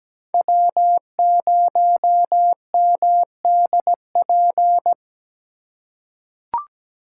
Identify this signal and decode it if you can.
Morse code, heard as a keyed tone, about 17 wpm, 700 Hz. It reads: W0MDP
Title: USA Call Signs - 4x Character Spacing - 17wpm